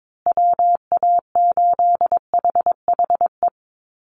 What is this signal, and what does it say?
Morse code, heard as a keyed tone, about 22 wpm, 700 Hz. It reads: WA8HHE
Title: USA Call Signs Repeated 2x 22wpm